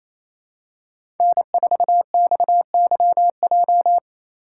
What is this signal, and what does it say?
Morse code, heard as a keyed tone, about 28 wpm, 700 Hz. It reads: N4XYJ
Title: USA Call Signs - 5 Characters - 6x Character Spacing - 28wpm